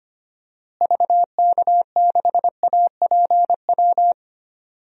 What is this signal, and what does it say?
Morse code, heard as a keyed tone, about 25 wpm, 700 Hz. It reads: VK6APW
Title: DXCC-100 Call Signs - 6 Characters - Repeated 3x - 25wpm